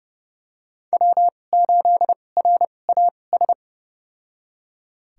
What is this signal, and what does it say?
Morse code, heard as a keyed tone, about 30 wpm, spaced as for 22 wpm, 700 Hz. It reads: W8RAS
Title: USA Call Signs - 5 Characters - 2x Character Spacing - 30wpm